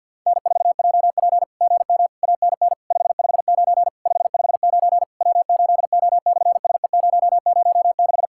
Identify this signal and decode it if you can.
Morse code, heard as a keyed tone, about 46 wpm, 690 Hz. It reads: N4JP GM ANN 559 559 W8O/SE006
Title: SOTA QSO Practice 50wpm